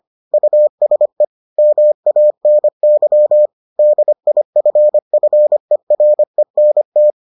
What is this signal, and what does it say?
Morse code, heard as a keyed tone, about 25 wpm, 600 Hz. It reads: USE MANY DIFFERENT